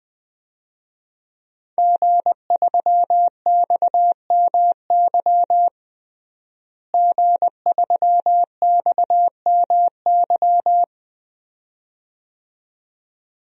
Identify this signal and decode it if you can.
Morse code, heard as a keyed tone, about 20 wpm, 700 Hz. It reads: G3XMY G3XMY
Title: DXCC-100 Call Signs - 5 Characters - Repeated 2x - 20wpm